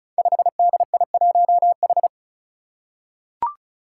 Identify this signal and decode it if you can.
Morse code, heard as a keyed tone, about 36 wpm, 700 Hz. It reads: 5DI1H
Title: Five Challenging Letter-Number Combinations 35wpm